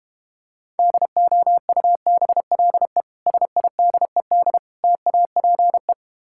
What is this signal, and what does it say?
Morse code, heard as a keyed tone, about 32 wpm, 700 Hz. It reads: DOUBLE SIDED TAPE